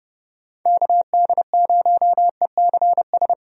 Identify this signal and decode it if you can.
Morse code, heard as a keyed tone, about 30 wpm, 700 Hz. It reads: KD0ECS
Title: USA Call Signs - 6 Characters - Repeated 3x - 30wpm